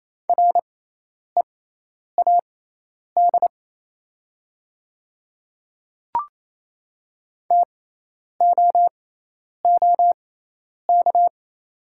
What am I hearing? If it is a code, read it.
Morse code, heard as a keyed tone, about 28 wpm, spaced as for 10 wpm, 700 Hz. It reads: READ TOOK
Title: 6x Character Spacing - Top 500 Words - 28wpm